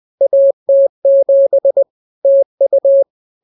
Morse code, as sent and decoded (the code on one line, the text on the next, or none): .- - --... / - ..-
AT7 TU